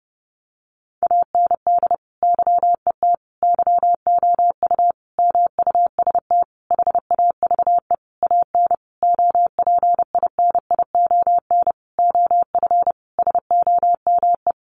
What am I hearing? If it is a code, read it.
Morse code, heard as a keyed tone, about 30 wpm, 700 Hz. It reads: AND YET YOU MUST HAVE AN OPINION OF SOME
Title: Sentences from Top 1300 Words 30wpm